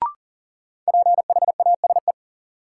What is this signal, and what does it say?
Morse code, heard as a keyed tone, about 40 wpm, 700 Hz. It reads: PHASE
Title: Sentences from Top 1200 Words - New Word Drill Alt - 40wpm